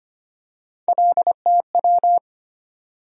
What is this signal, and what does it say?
Morse code, heard as a keyed tone, about 25 wpm, 700 Hz. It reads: LTW
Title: USA Call Sign Postfixes - 3 Letters 25wpm